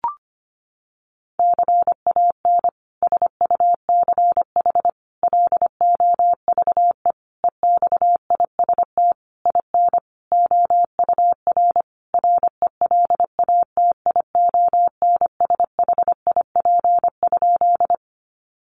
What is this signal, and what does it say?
Morse code, heard as a keyed tone, about 25 wpm, 700 Hz. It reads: CAN SUCH LOVE EXIST IN OUR RELATIONSHIP?
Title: Sentences from Top 1100 Words 25wpm